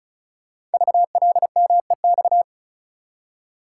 Morse code, heard as a keyed tone, about 36 wpm, 690 Hz. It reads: VLMEX
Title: Code Groups - 5 Letters - 35wpm